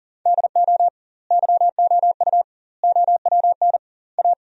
Morse code, heard as a keyed tone, about 40 wpm, 700 Hz. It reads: DO YOU OWN A
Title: 2x Word-Spacing - Sentences from Top 400 Words - 40wpm